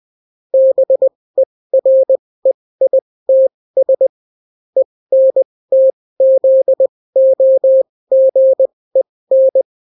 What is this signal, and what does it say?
Morse code, heard as a keyed tone, about 20 wpm, spaced as for 16 wpm, 530 Hz. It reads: BEREITS ENTZOGEN